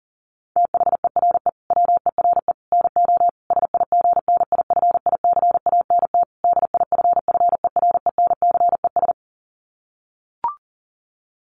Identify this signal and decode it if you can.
Morse code, heard as a keyed tone, about 40 wpm, 700 Hz. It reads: THERE WERE NO SIGNIFICANT DIFFERENCES
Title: Sets of 5 Words Encore2 40wpm